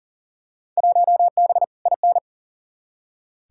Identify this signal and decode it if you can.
Morse code, heard as a keyed tone, about 40 wpm, 700 Hz. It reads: 1B IN
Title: Field Day Exchanges - 40wpm